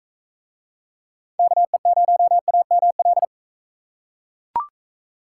Morse code, heard as a keyed tone, about 43 wpm, 700 Hz. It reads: KE0AML